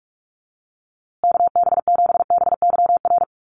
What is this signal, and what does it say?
Morse code, heard as a keyed tone, about 44 wpm, 700 Hz. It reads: KB7BYR
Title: USA Call Signs Repeated 2x 45wpm